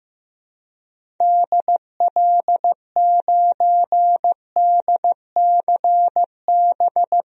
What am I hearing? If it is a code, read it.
Morse code, heard as a keyed tone, about 15 wpm, 700 Hz. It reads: DL9DCB
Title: DXCC-100 Call Signs - 6 Characters - 15wpm